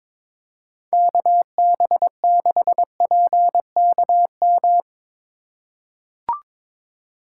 Morse code, heard as a keyed tone, about 22 wpm, 700 Hz. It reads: KB6PKM
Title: USA Call Signs - 6 Characters - 4x Character Spacing - 22wpm